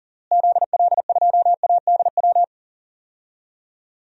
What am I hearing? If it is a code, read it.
Morse code, heard as a keyed tone, about 40 wpm, 700 Hz. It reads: ZL2ADW
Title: DXCC-100 Call Signs - 6 Characters - Repeated 3x - 40wpm